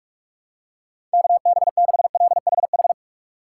Morse code, heard as a keyed tone, about 43 wpm, 700 Hz. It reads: KB6LHH
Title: USA Call Signs - 6 Characters - 2x Character Spacing - 45wpm